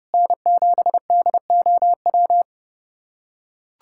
Morse code, heard as a keyed tone, about 32 wpm, 700 Hz. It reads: N7DOW
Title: USA Callsign DE Callsign - 5 Characters - 30wpm